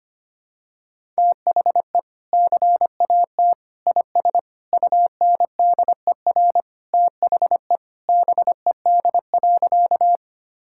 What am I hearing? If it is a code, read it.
Morse code, heard as a keyed tone, about 25 wpm, 700 Hz. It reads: THE CAT IS UNDER THE BED.
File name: practice / 25wpm / Sentences with 2-Word Phrases - 25wpm